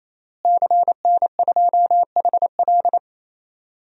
Morse code, heard as a keyed tone, about 27 wpm, 700 Hz. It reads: CN2HL
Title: DXCC-100 Call Signs - 5 Characters - Repeated 3x - 28wpm